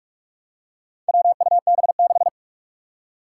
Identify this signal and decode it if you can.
Morse code, heard as a keyed tone, about 41 wpm, 700 Hz. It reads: WUB6
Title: Four Challenging Letter-Number Combinations 45wpm